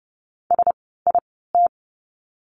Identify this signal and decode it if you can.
Morse code, heard as a keyed tone, about 30 wpm, spaced as for 17 wpm, 700 Hz. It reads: SIT
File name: course / 30:10wpm / 108 Lesson 18f - Letter m - Words 30wpm